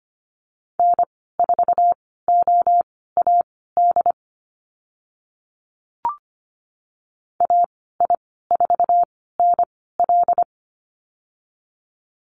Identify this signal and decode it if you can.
Morse code, heard as a keyed tone, about 25 wpm, spaced as for 16 wpm, 690 Hz. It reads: N4OAD AI4NL
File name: Lesson 15g - Number 5 - Call Signs 25wpm